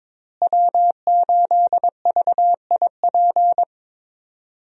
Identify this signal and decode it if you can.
Morse code, heard as a keyed tone, about 22 wpm, 700 Hz. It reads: W8VIP
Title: USA Call Signs - 5 Characters - Repeated 2x - 22wpm